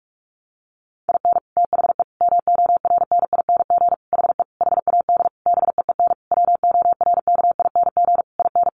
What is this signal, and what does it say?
Morse code, heard as a keyed tone, about 44 wpm, 700 Hz. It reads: IN THE MORNING HE HAD BEEN WORKING IN
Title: Sentences from Top 1200 Words 45wpm